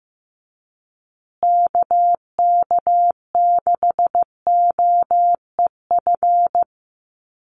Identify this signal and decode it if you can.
Morse code, heard as a keyed tone, about 15 wpm, 700 Hz. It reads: KK6OEF
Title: USA Call Signs Repeated 3x 15wpm